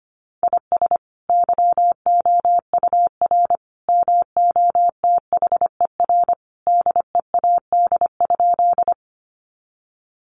Morse code, heard as a keyed tone, about 25 wpm, 700 Hz. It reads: IS YOUR MOTHER DEAD?
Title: Sentences from Top 1500 Words 25wpm